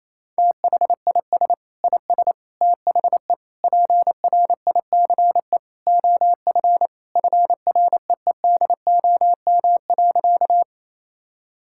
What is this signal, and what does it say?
Morse code, heard as a keyed tone, about 28 wpm, 700 Hz. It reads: THIS IS THE PRICE OF FREEDOM.
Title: Sentences from Top 1000 Words - Review Part 5 - 28wpm